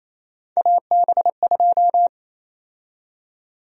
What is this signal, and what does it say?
Morse code, heard as a keyed tone, about 28 wpm, 700 Hz. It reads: AB2
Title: USA Call Sign Prefixes - 28wpm